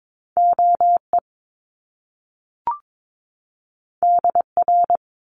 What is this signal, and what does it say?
Morse code, heard as a keyed tone, about 22 wpm, 700 Hz. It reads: OE DR
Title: Top Two Letter Combinations in Words 22wpm